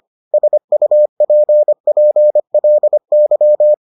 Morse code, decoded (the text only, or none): SUPPLY